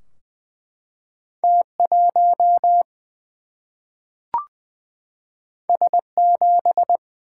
Morse code, heard as a keyed tone, about 20 wpm, 700 Hz. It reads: T 1 S 7
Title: Single-Letter Single-Number 20wpm